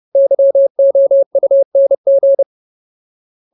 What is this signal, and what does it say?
Morse code, heard as a keyed tone, about 30 wpm, 550 Hz. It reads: YOUNG
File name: Ultimate Word List / 30WPM Ultimate